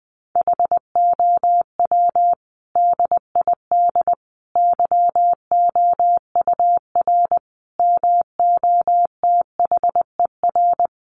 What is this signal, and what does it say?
Morse code, heard as a keyed tone, about 20 wpm, 700 Hz. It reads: HOW DID YOUR MOTHER
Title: Sentences from Top 700 Words 20wpm